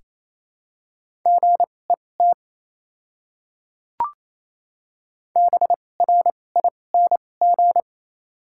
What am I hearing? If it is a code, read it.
Morse code, heard as a keyed tone, about 28 wpm, spaced as for 20 wpm, 700 Hz. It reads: GET BRING
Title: 2x Character Spacing - Top 500 Words - 28wpm